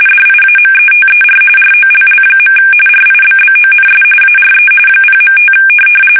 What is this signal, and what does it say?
RTTY, about 45.45 baud. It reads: DIAMETER ROLL. RIBBONS ARE 0.5
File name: fsk850 / teletype-corp